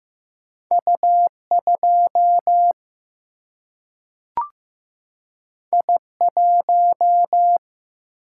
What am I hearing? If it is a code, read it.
Morse code, heard as a keyed tone, about 15 wpm, 700 Hz. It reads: U2 I1
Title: Single-Letter Single-Number 15wpm